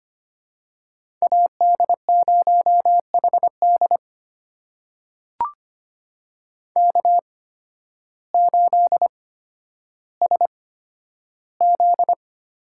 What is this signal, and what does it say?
Morse code, heard as a keyed tone, about 25 wpm, spaced as for 12 wpm, 700 Hz. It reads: AD0HD K8SZ
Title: USA Call Signs - 5 Characters - 8x Character Spacing - 25wpm